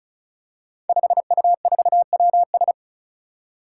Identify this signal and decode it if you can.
Morse code, heard as a keyed tone, about 36 wpm, 700 Hz. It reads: 5U4WS
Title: Five Challenging Letter-Number Combinations 35wpm